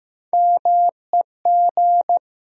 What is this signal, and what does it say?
Morse code, heard as a keyed tone, about 15 wpm, 700 Hz. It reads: MEG
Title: USA Common Names - Max 5 Letters - 15wpm